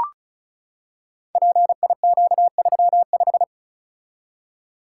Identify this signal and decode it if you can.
Morse code, heard as a keyed tone, about 35 wpm, 700 Hz. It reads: PIQ35
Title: Code Groups - 5 Letter:Numbers - 35wpm